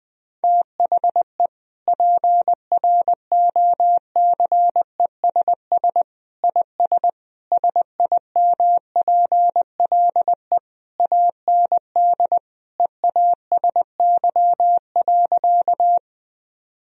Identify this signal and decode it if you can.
Morse code, heard as a keyed tone, about 21 wpm, 700 Hz. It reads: THE PROCESS IS SIMPLE AND EASY.
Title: Sentences from Top 1000 Words - Review Part 3 - 20wpm